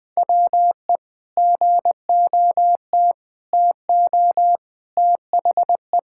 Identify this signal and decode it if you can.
Morse code, heard as a keyed tone, about 20 wpm, 700 Hz. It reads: WE GOT TO THE
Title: Sentences with 5-Word Phrases - 20wpm